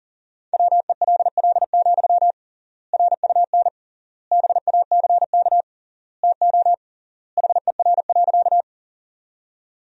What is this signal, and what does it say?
Morse code, heard as a keyed tone, about 40 wpm, 700 Hz. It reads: WELL, RUN BACK TO HER.
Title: 3x Word-Spacing - Sentences from Top 300 Words - 40wpm